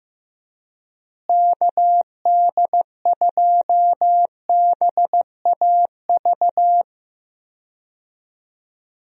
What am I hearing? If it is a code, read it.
Morse code, heard as a keyed tone, about 16 wpm, 700 Hz. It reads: KD2BAV
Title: USA Call Signs - 6 Characters - Repeated 3x - 15wpm